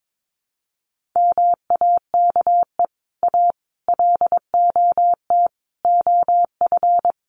Morse code, heard as a keyed tone, about 22 wpm, 700 Hz. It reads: MAKE A LOT OF
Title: Sets of 5 Words Encore2 22wpm